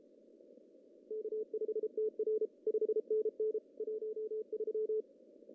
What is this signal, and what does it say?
Morse code, heard as a keyed tone, about 32 wpm, 410 Hz. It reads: K5TR 5NN 13
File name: CW / cqww-cw-k5tr